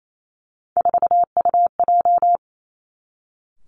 Morse code, heard as a keyed tone, about 28 wpm, 700 Hz. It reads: 4UJ
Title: Three Challenging Letter-Number Combinations 28wpm